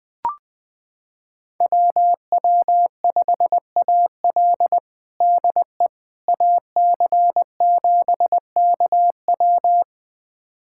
Morse code, heard as a keyed tone, about 20 wpm, 700 Hz. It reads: WW5AL DE AC7KW
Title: USA Callsign DE Callsign - 5 Characters - 20wpm